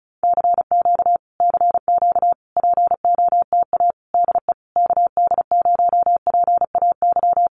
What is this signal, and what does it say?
Morse code, heard as a keyed tone, about 34 wpm, 700 Hz. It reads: CQ CQ POTA DE KD0PAY
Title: Common QSO Phrases Encore 35wpm